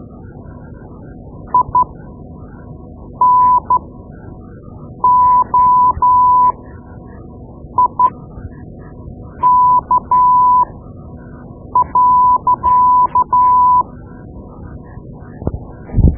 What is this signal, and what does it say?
Morse code, heard as a keyed tone, about 12 wpm, spaced as for 5 wpm, 1000 Hz. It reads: INOIK.
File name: Morse Lessons / Morse Lessons / Tape-2B